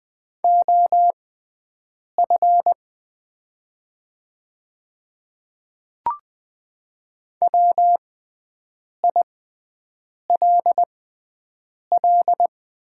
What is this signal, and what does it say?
Morse code, heard as a keyed tone, about 20 wpm, spaced as for 7 wpm, 700 Hz. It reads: OF WILL
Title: 6x Character Spacing - Top 100 Words - 20wpm